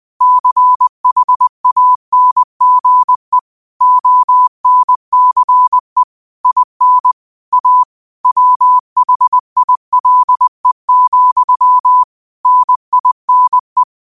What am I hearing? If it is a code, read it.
Morse code, heard as a keyed tone, about 20 wpm, 1000 Hz. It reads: CHANGE ONCE IN A WHILE, NINE
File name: qotdfiles / 20wpm